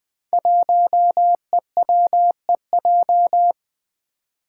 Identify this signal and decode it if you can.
Morse code, heard as a keyed tone, about 20 wpm, 700 Hz. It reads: 1EWEJ
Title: Five Challenging Letter-Number Combinations 20wpm